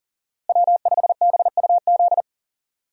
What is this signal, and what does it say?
Morse code, heard as a keyed tone, about 40 wpm, 700 Hz. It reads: W5BUZ
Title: USA Call Signs - 5 Characters - Repeated 2x - 40wpm